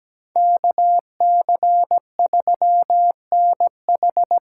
Morse code, heard as a keyed tone, about 20 wpm, 700 Hz. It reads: KC3NH
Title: Speed Racing - USA 2x2 Call Signs 17wpm